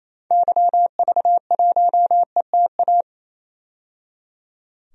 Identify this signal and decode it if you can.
Morse code, heard as a keyed tone, about 28 wpm, 700 Hz. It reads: YV1ETA
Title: DXCC-100 Call Signs - 6 Characters - 28wpm